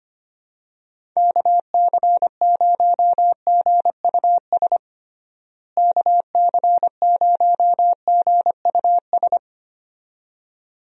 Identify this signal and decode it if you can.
Morse code, heard as a keyed tone, about 25 wpm, 700 Hz. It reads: KC0GUS KC0GUS
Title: USA Call Signs - 6 Characters - Repeated 2x - 25wpm